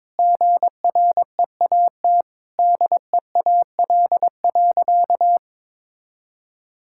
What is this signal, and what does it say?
Morse code, heard as a keyed tone, about 22 wpm, 700 Hz. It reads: GREAT DEAL.
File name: practice / 22wpm / Sentences with 3-Word Phrases - 22wpm